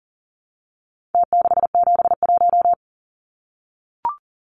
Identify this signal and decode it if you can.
Morse code, heard as a keyed tone, about 40 wpm, 700 Hz. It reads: T671
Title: Numbers - Contest Style - 4-Digit 40wpm